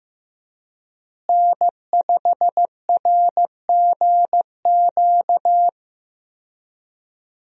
Morse code, heard as a keyed tone, about 16 wpm, 700 Hz. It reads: N5RGQ
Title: USA Call Signs - 5 Characters - Repeated 3x - 15wpm